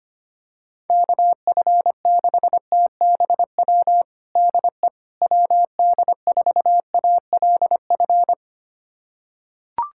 Morse code, heard as a keyed tone, about 25 wpm, 700 Hz. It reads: KF6TBW DE WD4ALF
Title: USA Callsign DE Callsign - 6 Characters - 25wpm